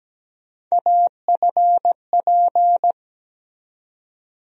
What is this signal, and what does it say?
Morse code, heard as a keyed tone, about 17 wpm, 700 Hz. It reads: AFP
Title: USA Call Sign Postfixes - 3 Letters 17wpm